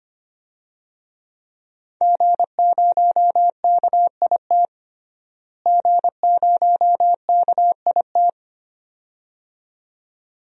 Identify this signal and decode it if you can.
Morse code, heard as a keyed tone, about 25 wpm, 700 Hz. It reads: G0KIT G0KIT
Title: DXCC-100 Call Signs - 5 Characters - Repeated 2x - 25wpm